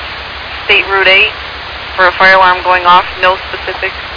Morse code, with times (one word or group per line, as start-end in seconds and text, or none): none